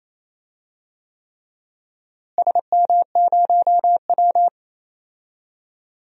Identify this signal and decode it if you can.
Morse code, heard as a keyed tone, about 28 wpm, 700 Hz. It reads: SM0W
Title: DXCC-100 Call Signs - 4 Characters - Repeated 3x - 28wpm